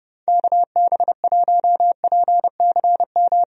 Morse code, heard as a keyed tone, about 32 wpm, 700 Hz. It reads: KB1PCM
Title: USA Call Signs - 6 Characters - Repeated 3x - 30wpm